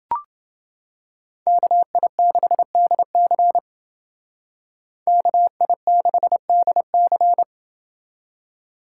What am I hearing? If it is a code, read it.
Morse code, heard as a keyed tone, about 28 wpm, 700 Hz. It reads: KI6DC KI6DC